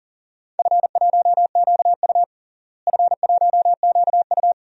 Morse code, heard as a keyed tone, about 40 wpm, 700 Hz. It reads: F1QU F1QU
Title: DXCC-100 Call Signs - 4 Characters - Repeated 2x - 40wpm